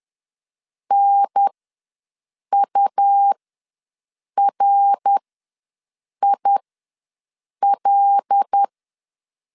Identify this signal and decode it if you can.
Morse code, heard as a keyed tone, about 10 wpm, spaced as for 6 wpm, 800 Hz. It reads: NURIL
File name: LEZIONI / 5 LBVC / 5 LB ws 10wpm 3616